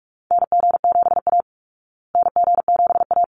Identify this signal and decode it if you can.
Morse code, heard as a keyed tone, about 44 wpm, 700 Hz. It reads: NG7A NG7A
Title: Call Signs Repeated 3x - 45wpm